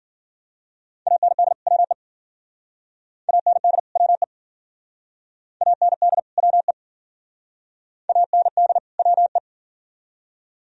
Morse code, heard as a keyed tone, about 50 wpm, 700 Hz. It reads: AND WE AND WE AND WE AND WE
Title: Speed-Racing - Sets of 2 Words 40wpm